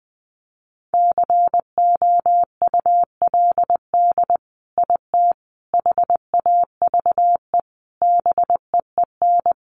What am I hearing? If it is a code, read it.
Morse code, heard as a keyed tone, about 20 wpm, 700 Hz. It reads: COULD IT HAVE BEEN
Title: Sentences from Top 1200 Words 20wpm